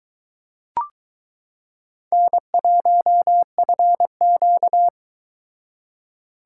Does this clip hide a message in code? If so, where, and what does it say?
Yes: Morse code from 2.1 to 4.9 s, heard as a keyed tone, about 22 wpm, 700 Hz. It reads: N1FQ